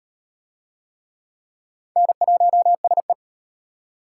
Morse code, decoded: N1SE

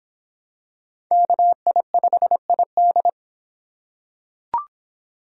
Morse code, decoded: KI5ID

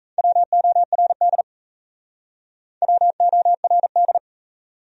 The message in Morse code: WORD WORD

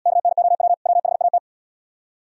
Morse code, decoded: KNOW WHAT

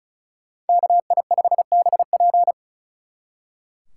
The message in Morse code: KI5BP

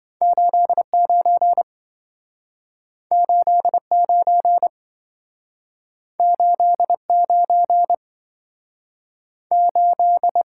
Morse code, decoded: 89 89 89 8

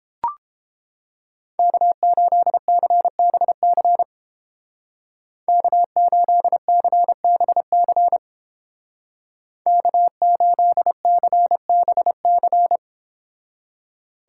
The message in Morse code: K8CBC K8CBC K8CBC

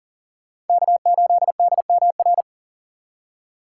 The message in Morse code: K8DMR

